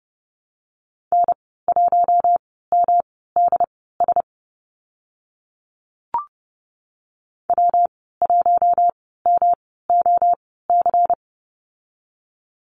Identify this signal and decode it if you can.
Morse code, heard as a keyed tone, about 30 wpm, spaced as for 17 wpm, 700 Hz. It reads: N1MDS W1MOC
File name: Lesson 19g - Letter w - Call Signs 30wpm